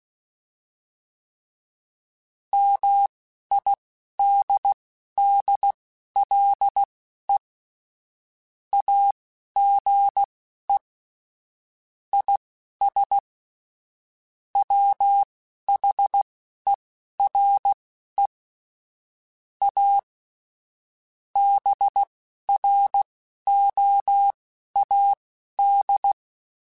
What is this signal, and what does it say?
Morse code, heard as a keyed tone, about 16 wpm, spaced as for 11 wpm, 800 Hz. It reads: MIDDLE AGE IS WHERE A BROAD
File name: Chuckles / Chuckles-mp3s / MiddleAge @10